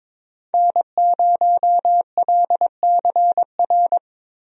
Morse code, heard as a keyed tone, about 21 wpm, 700 Hz. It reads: N0LCR
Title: USA Call Signs - 5 Characters - Repeated 2x - 22wpm